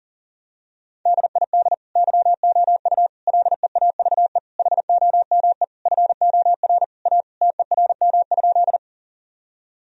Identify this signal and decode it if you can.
Morse code, heard as a keyed tone, about 40 wpm, 700 Hz. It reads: DID YOU LEAVE HOME FOR A TERM?
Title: Sentences from Top 1000 Words - Review Part 4 - 40wpm